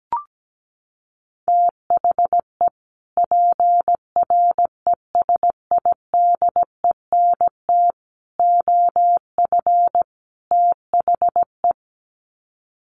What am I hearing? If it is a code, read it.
Morse code, heard as a keyed tone, about 17 wpm, 700 Hz. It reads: THE PRESIDENT OF THE
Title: Sets of 4 Words 17wpm